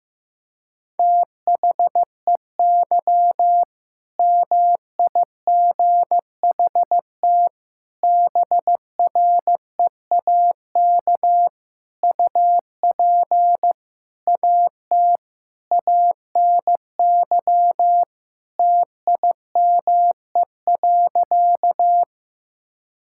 Morse code, heard as a keyed tone, about 16 wpm, 700 Hz. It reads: THEY MIGHT BREAK UP AT ANY TIME.
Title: Sentences from Top 1000 Words - Review Part 6 - 15wpm